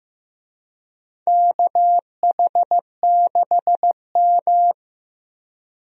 Morse code, heard as a keyed tone, about 16 wpm, 700 Hz. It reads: KH6M
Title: USA Call Signs - 4 Characters - Repeated 2x - 15wpm